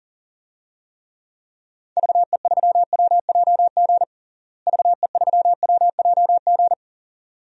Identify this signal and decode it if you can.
Morse code, heard as a keyed tone, about 37 wpm, 700 Hz. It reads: VE3WJG VE3WJG
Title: DXCC-100 Call Signs - 6 Characters - Repeated 2x - 40wpm